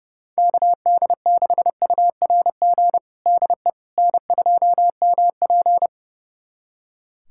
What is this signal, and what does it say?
Morse code, heard as a keyed tone, about 32 wpm, 700 Hz. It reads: KD6URG DE N2MP
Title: USA Callsign DE Callsign 30wpm